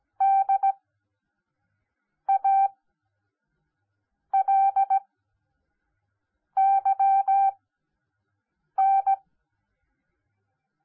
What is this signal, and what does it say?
Morse code, heard as a keyed tone, about 17 wpm, spaced as for 5 wpm, 780 Hz. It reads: DALYN
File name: Onlinelehrgang / morsen / DJ4UF Morsen10